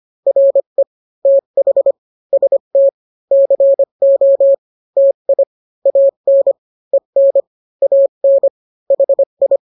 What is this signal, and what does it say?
Morse code, heard as a keyed tone, about 25 wpm, 550 Hz. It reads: RE TH ST CO TI AN EN AN HI